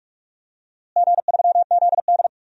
Morse code, heard as a keyed tone, about 46 wpm, 700 Hz. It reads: G3ZD
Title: DXCC-100 Call Signs - 4 Characters - Repeated 3x - 45wpm